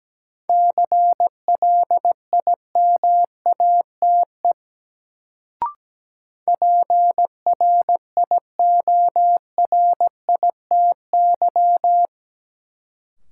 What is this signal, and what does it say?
Morse code, heard as a keyed tone, about 17 wpm, 700 Hz. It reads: CLIMATE PRIORITY
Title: Sentences from Top 1600 Words - New Word Drill - 17wpm